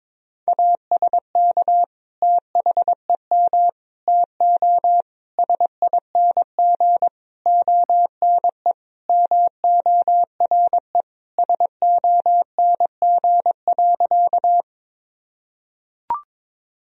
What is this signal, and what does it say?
Morse code, heard as a keyed tone, about 22 wpm, 700 Hz. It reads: ASK THEM TO SING ONE MORE SONG.
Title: Sentences from Top 1000 Words - Review Part 2 - 22wpm